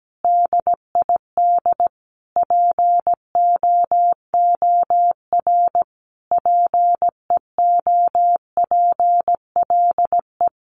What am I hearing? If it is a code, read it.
Morse code, heard as a keyed tone, about 17 wpm, 700 Hz. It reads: DID POOR PEOPLE
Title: Sentences from Top 1100 Words 17wpm